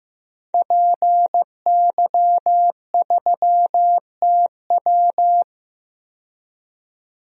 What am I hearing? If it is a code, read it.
Morse code, heard as a keyed tone, about 15 wpm, 700 Hz. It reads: PY3TW
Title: DXCC-100 Call Signs - 5 Characters - 15wpm